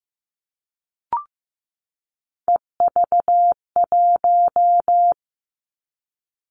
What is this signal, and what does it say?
Morse code, heard as a keyed tone, about 15 wpm, 700 Hz. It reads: EV1
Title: Three Challenging Letter-Number Combinations 15wpm